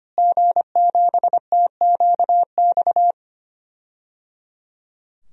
Morse code, heard as a keyed tone, about 25 wpm, 700 Hz. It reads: G7TQX
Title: DXCC-100 Call Signs - 5 Characters - Repeated 3x - 25wpm